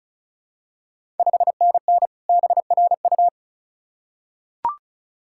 Morse code, heard as a keyed tone, about 36 wpm, 700 Hz. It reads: 5NN BRU
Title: Ontario QSO Party Exchange 35wpm